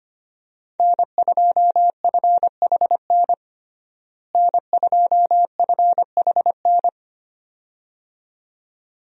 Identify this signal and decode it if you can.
Morse code, heard as a keyed tone, about 25 wpm, 700 Hz. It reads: N2FHN N2FHN
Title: USA Call Signs - 5 Characters - Repeated 2x - 25wpm